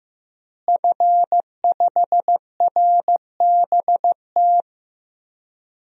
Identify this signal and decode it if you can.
Morse code, heard as a keyed tone, about 16 wpm, 700 Hz. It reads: F5RBT
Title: DXCC-100 Call Signs - 5 Characters - Repeated 3x - 15wpm